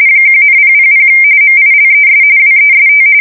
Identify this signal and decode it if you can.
RTTY, about 45.45 baud. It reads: 1,000 MILES WEST
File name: fsk170 / battleshipnj